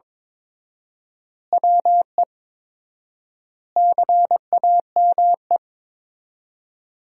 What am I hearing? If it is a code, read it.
Morse code, heard as a keyed tone, about 22 wpm, 700 Hz. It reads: WE CAME
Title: 4x Word-Spacing - Sentences from Top 400 Words - 22wpm